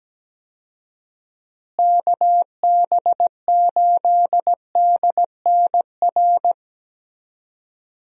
Morse code, heard as a keyed tone, about 17 wpm, 700 Hz. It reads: KB8DNR
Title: USA Call Signs - 6 Characters - Repeated 2x - 17wpm